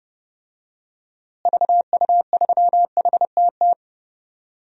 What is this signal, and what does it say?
Morse code, heard as a keyed tone, about 30 wpm, 700 Hz. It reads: VU3HTT
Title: DXCC-100 Call Signs - 6 Characters - Repeated 2x - 30wpm